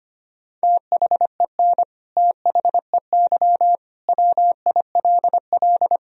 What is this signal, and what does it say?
Morse code, heard as a keyed tone, about 25 wpm, 700 Hz. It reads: THEN THEY WILL
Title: Sentences from Top 200 Words and Abbreviations 25wpm